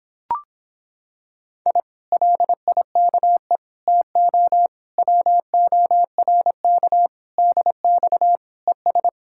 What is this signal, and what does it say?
Morse code, heard as a keyed tone, about 26 wpm, 700 Hz. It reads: I LIKE TO WORK DX ES